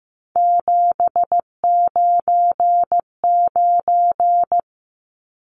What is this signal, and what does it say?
Morse code, heard as a keyed tone, about 15 wpm, 700 Hz. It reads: 799
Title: Numbers - 3-Digit 15wpm